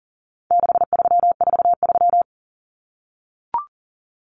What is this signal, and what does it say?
Morse code, heard as a keyed tone, about 40 wpm, 700 Hz. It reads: 6343
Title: Common Cut Numbers - 4-Digit 40wpm